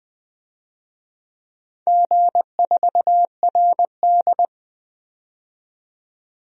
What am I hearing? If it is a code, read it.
Morse code, heard as a keyed tone, about 21 wpm, 700 Hz. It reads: G4RD
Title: DXCC-100 Call Signs - 4 Characters - 20wpm